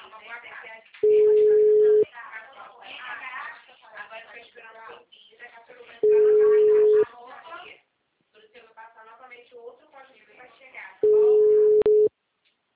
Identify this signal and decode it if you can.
Ringback tone (Europe, 425 Hz), with speech.